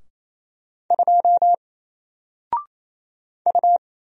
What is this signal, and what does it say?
Morse code, heard as a keyed tone, about 27 wpm, 700 Hz. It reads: 2 U
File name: Single-Letter-Number-28wpm